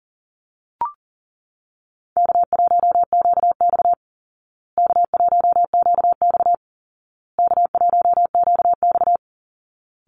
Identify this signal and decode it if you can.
Morse code, heard as a keyed tone, about 40 wpm, 700 Hz. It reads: K1QX K1QX K1QX
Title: Easy Call Signs Repeated 3x - 40wpm